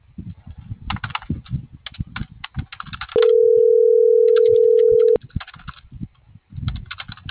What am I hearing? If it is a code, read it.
Ringback tone (North America), 440 + 480 Hz.